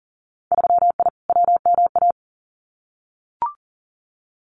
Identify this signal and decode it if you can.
Morse code, heard as a keyed tone, about 41 wpm, 700 Hz. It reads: 3I WMA
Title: Winter Field Day Exchanges Encore 40wpm